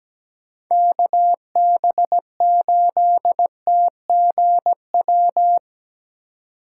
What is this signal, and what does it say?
Morse code, heard as a keyed tone, about 20 wpm, 700 Hz. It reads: KB8TGW
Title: Speed Racing - USA 2x3 Call Signs 17wpm